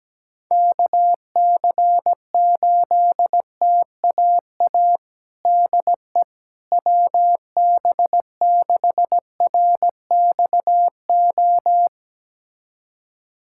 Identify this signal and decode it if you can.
Morse code, heard as a keyed tone, about 17 wpm, 700 Hz. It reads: KC8TAA DE WB6RXO
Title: USA Callsign DE Callsign - 6 Characters - 17wpm